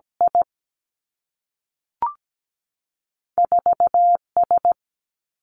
Morse code, heard as a keyed tone, about 17 wpm, 700 Hz. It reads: I 4 S
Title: Two Challenging Letter-Number Combinations 17wpm